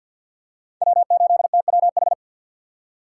Morse code, heard as a keyed tone, about 51 wpm, 700 Hz. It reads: W8TWH
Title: USA Call Signs - 5 Characters - 3x Character Spacing - 50wpm